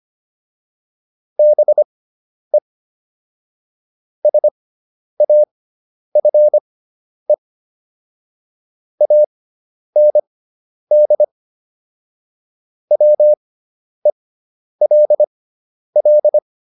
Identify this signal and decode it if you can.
Morse code, heard as a keyed tone, about 25 wpm, spaced as for 10 wpm, 600 Hz. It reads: BE SAFE AND WELL